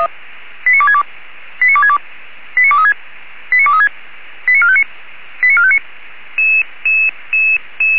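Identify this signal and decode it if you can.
Five-tone selective call (ZVEI), heard as quick runs of five tones, tones of about 70 ms. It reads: 89262 89262 89236 89236 89459 89459
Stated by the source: ZVEI five-tone call received via SDR.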